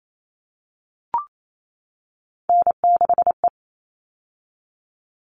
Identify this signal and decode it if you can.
Morse code, heard as a keyed tone, about 28 wpm, 700 Hz. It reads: N6E
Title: All Cut Numbers - 3-Digit 28wpm